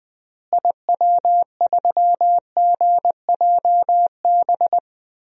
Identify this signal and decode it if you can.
Morse code, heard as a keyed tone, about 20 wpm, 700 Hz. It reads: IW3GJB
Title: DXCC-100 Call Signs - 6 Characters - Repeated 2x - 20wpm